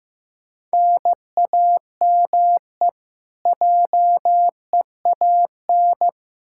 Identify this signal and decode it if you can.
Morse code, heard as a keyed tone, about 15 wpm, 700 Hz. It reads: NAME JEAN
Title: Practice QSO - Exchange 1 - Beginning - 15wpm